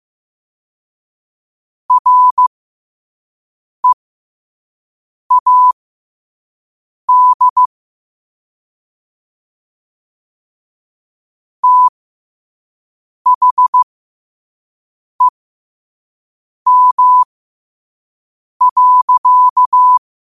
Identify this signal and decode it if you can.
Morse code, heard as a keyed tone, about 15 wpm, spaced as for 5 wpm, 1000 Hz. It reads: READ THEM.